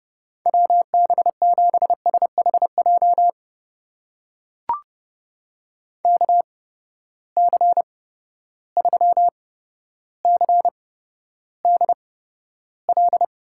Morse code, heard as a keyed tone, about 32 wpm, spaced as for 13 wpm, 700 Hz. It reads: WB7SHJ KC3CDL
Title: USA Call Signs - 6 Characters - 8x Character Spacing - 30wpm